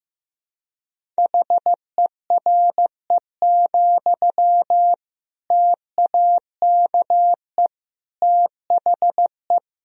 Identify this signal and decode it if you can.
Morse code, heard as a keyed tone, about 15 wpm, 700 Hz. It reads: HERE, TAKE THE
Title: Sentences from Top 1000 Words - Review Part 1 - 15wpm